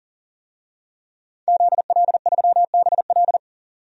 Morse code, heard as a keyed tone, about 40 wpm, 700 Hz. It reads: ZL3BL
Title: DXCC-100 Call Signs - 5 Characters - 40wpm